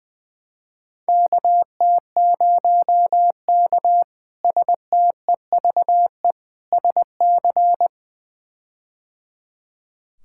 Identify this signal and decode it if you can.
Morse code, heard as a keyed tone, about 20 wpm, 700 Hz. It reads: KT0K STEVE SC
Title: K1USN SST QSO Practice - 20wpm